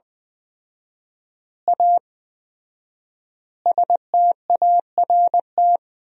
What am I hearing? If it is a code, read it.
Morse code, heard as a keyed tone, about 20 wpm, 700 Hz. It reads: A START
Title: 4x Word-Spacing - Sentences from Top 300 Words - 20wpm